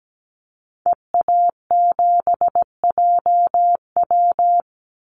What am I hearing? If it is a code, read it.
Morse code, heard as a keyed tone, about 17 wpm, 700 Hz. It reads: EA7JW
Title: Call Signs Repeated 2x - 17wpm